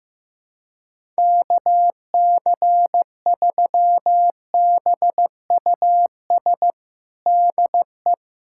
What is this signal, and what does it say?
Morse code, heard as a keyed tone, about 15 wpm, 700 Hz. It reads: KC3BUS DE
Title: USA Callsign DE Callsign 15wpm